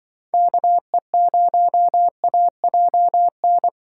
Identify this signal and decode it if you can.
Morse code, heard as a keyed tone, about 25 wpm, 700 Hz. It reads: KE0AJN